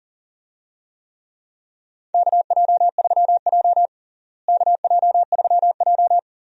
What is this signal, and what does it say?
Morse code, heard as a keyed tone, about 40 wpm, 700 Hz. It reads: KJ3J KJ3J
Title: USA Call Signs - 4 Characters - Repeated 3x - 40wpm